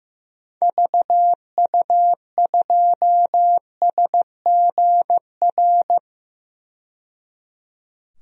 Morse code, heard as a keyed tone, about 16 wpm, 700 Hz. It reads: VU2SGR
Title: DXCC-100 Call Signs - 6 Characters - 15wpm